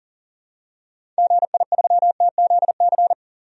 Morse code, heard as a keyed tone, about 40 wpm, 700 Hz. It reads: GI3TZC
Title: DXCC-100 Call Signs - 6 Characters - 40wpm